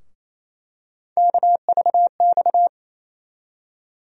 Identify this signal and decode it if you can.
Morse code, heard as a keyed tone, about 28 wpm, 700 Hz. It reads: KVX
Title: USA Call Sign Postfixes - 3 Letters 28wpm